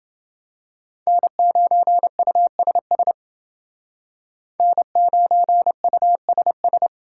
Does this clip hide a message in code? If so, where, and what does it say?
Yes: Morse code from 1.1 to 6.9 s, heard as a keyed tone, about 28 wpm, 700 Hz. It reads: N9USS N9USS